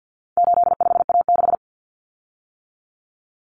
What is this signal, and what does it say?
Morse code, heard as a keyed tone, about 50 wpm, 700 Hz. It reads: 85A6
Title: Common Cut Numbers - 4-Digit 50wpm